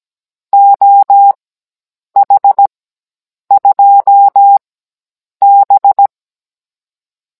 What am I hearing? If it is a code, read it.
Morse code, heard as a keyed tone, about 17 wpm, spaced as for 8 wpm, 800 Hz. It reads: OH2B
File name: Cours-NR19-16mm